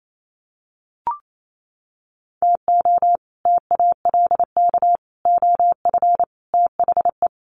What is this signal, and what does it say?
Morse code, heard as a keyed tone, about 28 wpm, 700 Hz. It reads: TO TALK OF THE
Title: Sets of 5 Words 28wpm